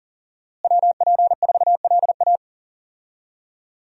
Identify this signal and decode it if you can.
Morse code, heard as a keyed tone, about 40 wpm, 690 Hz. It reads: WP4LA
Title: USA Call Signs - 5 Characters - Repeated 3x - 40wpm